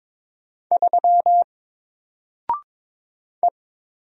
Morse code, heard as a keyed tone, about 22 wpm, 700 Hz. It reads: 3 E
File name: Single-Letter-Number-22wpm